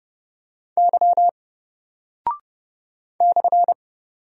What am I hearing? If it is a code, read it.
Morse code, heard as a keyed tone, about 30 wpm, 700 Hz. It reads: Y /
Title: Single Character (Letter:Number:Punctuation:Prosign) 30wpm